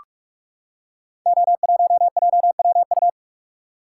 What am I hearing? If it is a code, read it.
Morse code, heard as a keyed tone, about 44 wpm, 690 Hz. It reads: O1JWU
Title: Code Groups - 5 Letter:Numbers - 45wpm